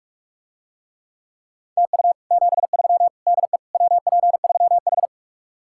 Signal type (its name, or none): Morse code